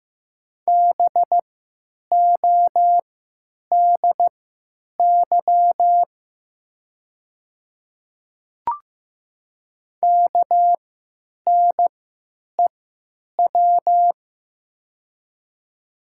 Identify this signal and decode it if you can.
Morse code, heard as a keyed tone, about 15 wpm, spaced as for 9 wpm, 700 Hz. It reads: BODY KNEW
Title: 3x Character Spacing - Top 500 Words - 15wpm